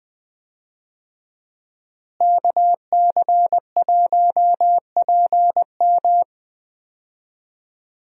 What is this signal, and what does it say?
Morse code, heard as a keyed tone, about 21 wpm, 700 Hz. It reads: KC1PM
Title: USA Call Signs - 5 Characters - Repeated 2x - 20wpm